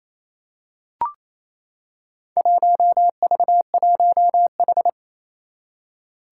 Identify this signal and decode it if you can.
Morse code, heard as a keyed tone, about 28 wpm, 700 Hz. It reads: 1V1H
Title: Four Challenging Letter-Number Combinations 28wpm